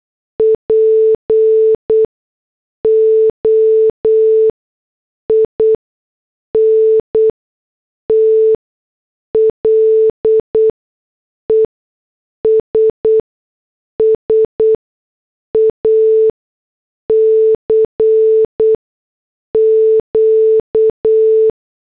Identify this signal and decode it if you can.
Morse code, heard as a keyed tone, about 8 wpm, spaced as for 6 wpm, 440 Hz. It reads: POINTLESS ACQ
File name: morse / headlines